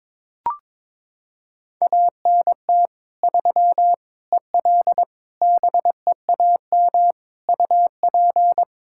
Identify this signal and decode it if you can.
Morse code, heard as a keyed tone, about 22 wpm, 700 Hz. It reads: ANT 3 EL BEAM UP
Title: Practice QSO - Exchange 2 - Info - 22wpm